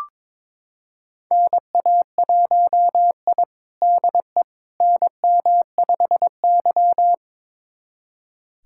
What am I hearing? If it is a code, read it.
Morse code, heard as a keyed tone, about 21 wpm, 700 Hz. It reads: NA1I DE NM5Y
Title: USA Callsign DE Callsign - 4 Characters - 22wpm